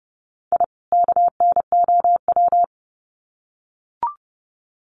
Morse code, heard as a keyed tone, about 30 wpm, 700 Hz. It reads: I KNOW
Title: Sets of 2 Words 30wpm